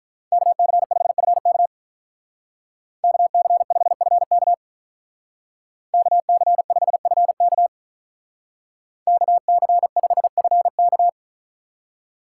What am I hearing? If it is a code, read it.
Morse code, heard as a keyed tone, about 43 wpm, 700 Hz. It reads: KC5FK KC5FK KC5FK KC5FK
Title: Speed Racing - USA 2x2 Call Signs 35wpm